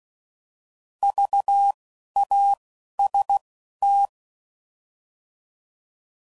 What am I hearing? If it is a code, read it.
Morse code, heard as a keyed tone, about 16 wpm, spaced as for 11 wpm, 800 Hz. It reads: VAST
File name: Prefixes / By First Letter / Vs